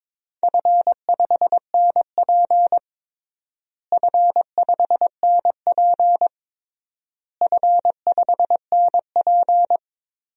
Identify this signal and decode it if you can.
Morse code, heard as a keyed tone, about 22 wpm, 700 Hz. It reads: F5NP F5NP F5NP
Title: DXCC-100 Call Signs - 4 Characters - Repeated 3x - 22wpm